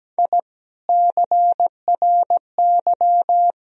Morse code, heard as a keyed tone, about 17 wpm, 700 Hz. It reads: I CRY
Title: Sentences from Top 1000 Words - Review Part 2 - 17wpm